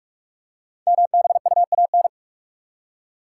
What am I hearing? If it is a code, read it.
Morse code, heard as a keyed tone, about 44 wpm, 700 Hz. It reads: MBUAN
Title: Code Groups - 5 Letters - 45wpm